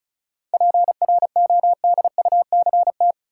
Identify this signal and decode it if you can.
Morse code, heard as a keyed tone, about 36 wpm, 700 Hz. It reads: PRODUCT
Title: Sentences from Top 1000 Words - Review Part 2 - New Word Drill - 35wpm